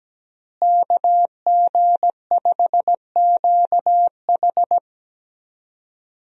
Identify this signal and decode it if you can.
Morse code, heard as a keyed tone, about 17 wpm, 700 Hz. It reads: KG5QH
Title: USA Call Signs - 5 Characters - Repeated 3x - 17wpm